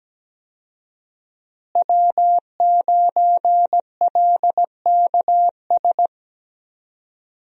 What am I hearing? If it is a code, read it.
Morse code, heard as a keyed tone, about 17 wpm, 700 Hz. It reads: W9LKS
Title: USA Call Signs - 5 Characters - Repeated 2x - 17wpm